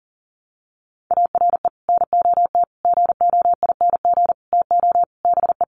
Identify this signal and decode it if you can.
Morse code, heard as a keyed tone, about 40 wpm, 700 Hz. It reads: ARE NOT GOING TO BE
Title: Sets of 5 Words Encore 40wpm